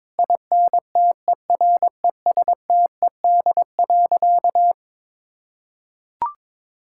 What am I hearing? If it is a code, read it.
Morse code, heard as a keyed tone, about 21 wpm, 700 Hz. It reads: INTERESTED.
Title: Sentences from Top 1000 Words - Review Part 6 - 22wpm